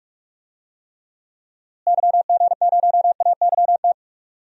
Morse code, heard as a keyed tone, about 42 wpm, 700 Hz. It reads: YG0AYT
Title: DXCC-100 Call Signs - 6 Characters - Repeated 3x - 45wpm